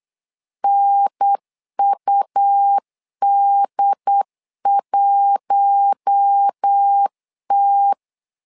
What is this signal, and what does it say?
Morse code, heard as a keyed tone, about 8 wpm, 800 Hz. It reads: NUD1T
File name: LEZIONI / 5 LBVC / 5 L 8wpm 2831